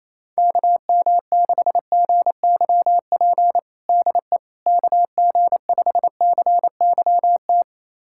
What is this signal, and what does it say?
Morse code, heard as a keyed tone, about 27 wpm, 700 Hz. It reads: KM6GYP DE KG5CYT
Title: USA Callsign DE Callsign - 6 Characters - 28wpm